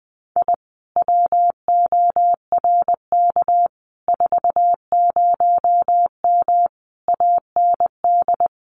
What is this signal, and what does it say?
Morse code, heard as a keyed tone, about 20 wpm, 700 Hz. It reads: I WORK 40M AND
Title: Common QSO Phrases Encore 20wpm